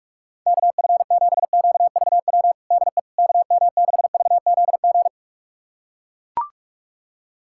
Morse code, heard as a keyed tone, about 44 wpm, 690 Hz. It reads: KF7QVW DE KM6VZG
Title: USA Callsign DE Callsign 45wpm